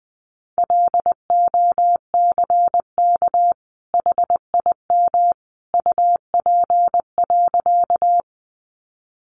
Morse code, heard as a keyed tone, about 20 wpm, 700 Hz. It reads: LOCK HIM UP.
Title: Sentences from Top 1300 Words 20wpm